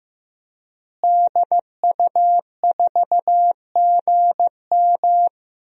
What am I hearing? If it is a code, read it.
Morse code, heard as a keyed tone, about 15 wpm, 700 Hz. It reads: DU4GM
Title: DXCC-100 Call Signs - 5 Characters - Repeated 3x - 15wpm